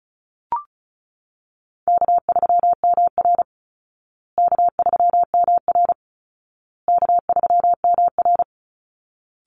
Morse code, heard as a keyed tone, about 34 wpm, 700 Hz. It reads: K3MR K3MR K3MR
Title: Easy Call Signs Repeated 3x - 35wpm